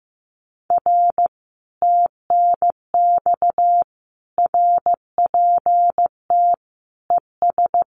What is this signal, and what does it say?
Morse code, heard as a keyed tone, about 15 wpm, 700 Hz. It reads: R TNX RPT ES
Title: Common QSO Phrases Encore 15wpm